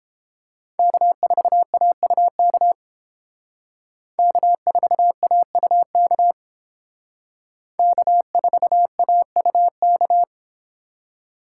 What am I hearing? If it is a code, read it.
Morse code, heard as a keyed tone, about 30 wpm, 700 Hz. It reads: K4AUK K4AUK K4AUK